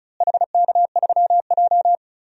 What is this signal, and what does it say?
Morse code, heard as a keyed tone, about 34 wpm, 690 Hz. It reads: HK3J
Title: DXCC-100 Call Signs - 4 Characters - Repeated 2x - 35wpm